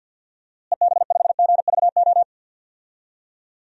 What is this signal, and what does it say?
Morse code, heard as a keyed tone, about 47 wpm, 700 Hz. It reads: EB5GVO
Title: DXCC-100 Call Signs - 6 Characters - Repeated 3x - 50wpm